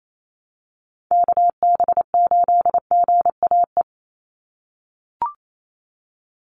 Morse code, heard as a keyed tone, about 28 wpm, 700 Hz. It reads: KB8GAE
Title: Call Signs - 28wpm